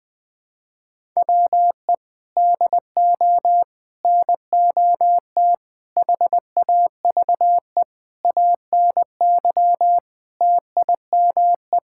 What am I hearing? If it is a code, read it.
Morse code, heard as a keyed tone, about 20 wpm, 700 Hz. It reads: WE DO NOT HAVE ANY TIME
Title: Sentences with 4-Word Phrases - 20wpm